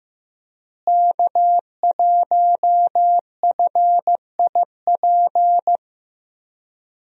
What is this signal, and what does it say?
Morse code, heard as a keyed tone, about 15 wpm, 700 Hz. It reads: K1FIP
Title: USA Call Signs - 5 Characters - Repeated 3x - 15wpm